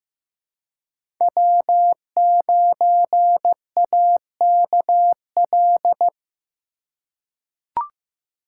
Morse code, heard as a keyed tone, about 16 wpm, 700 Hz. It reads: W9AKL
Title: USA Call Signs - 5 Characters - 3x Character Spacing - 15wpm